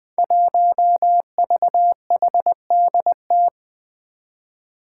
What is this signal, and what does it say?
Morse code, heard as a keyed tone, about 20 wpm, 700 Hz. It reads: 1VHDT
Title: Five Challenging Letter-Number Combinations 20wpm